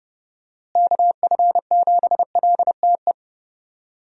Morse code, heard as a keyed tone, about 30 wpm, 700 Hz. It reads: KF7LTE
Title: USA Call Signs - 6 Characters - Repeated 2x - 30wpm